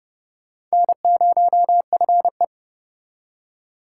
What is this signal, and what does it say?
Morse code, heard as a keyed tone, about 37 wpm, 700 Hz. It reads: N0FE
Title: Speed Racing - USA 1x2 Call Signs 30wpm